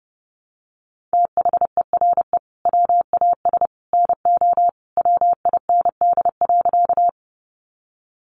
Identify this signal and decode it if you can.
Morse code, heard as a keyed tone, about 30 wpm, 700 Hz. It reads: THERE WAS NO WIND.
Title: Sentences from Top 400 Words 30wpm